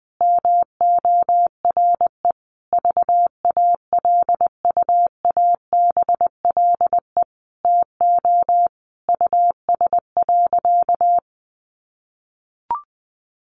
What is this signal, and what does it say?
Morse code, heard as a keyed tone, about 20 wpm, 700 Hz. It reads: MORE VALUABLE TO US.
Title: Sentences from Top 1100 Words 20wpm